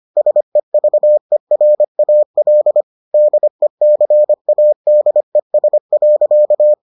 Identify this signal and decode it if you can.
Morse code, heard as a keyed tone, about 25 wpm, 600 Hz. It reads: SEVERAL DECADES.